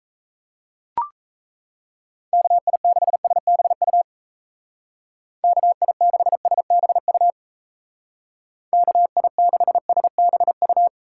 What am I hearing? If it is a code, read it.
Morse code, heard as a keyed tone, about 34 wpm, 700 Hz. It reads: KI6SBU KI6SBU KI6SBU
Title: Speed Racing - USA 2x3 Call Signs 28wpm